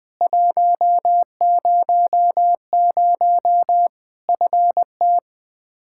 Morse code, heard as a keyed tone, about 20 wpm, 700 Hz. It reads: 100 FT
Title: Practice QSO - Exchange 2 - Info - 20wpm